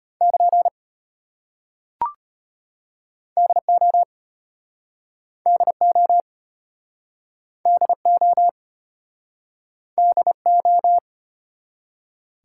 Morse code, heard as a keyed tone, about 32 wpm, 700 Hz. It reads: <KN> DO DO DO DO